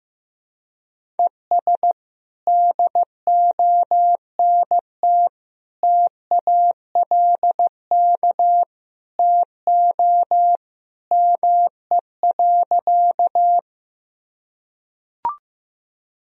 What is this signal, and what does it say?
Morse code, heard as a keyed tone, about 16 wpm, 700 Hz. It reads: ES DONT TALK TO ME.
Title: Sentences from Top 400 Words and Abbreviations 15wpm